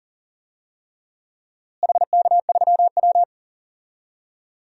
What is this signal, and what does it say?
Morse code, heard as a keyed tone, about 40 wpm, 700 Hz. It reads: HK3W
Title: DXCC-100 Call Signs - 4 Characters - Repeated 3x - 40wpm